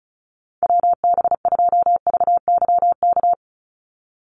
Morse code, heard as a keyed tone, about 36 wpm, 700 Hz. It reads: WB2VYK
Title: Call Signs - 35wpm